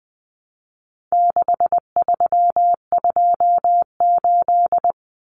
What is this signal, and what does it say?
Morse code, heard as a keyed tone, about 20 wpm, 700 Hz. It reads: 6328
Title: Common Cut Numbers - 4-Digit 20wpm